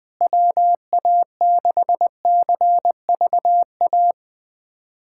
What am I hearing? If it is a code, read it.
Morse code, heard as a keyed tone, about 20 wpm, 700 Hz. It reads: WA6CVA
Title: USA Callsign DE Callsign - 6 Characters - 20wpm